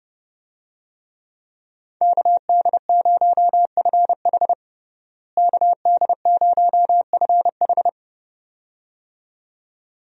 Morse code, heard as a keyed tone, about 29 wpm, 700 Hz. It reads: KD0FH KD0FH
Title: USA Call Signs - 5 Characters - Repeated 2x - 30wpm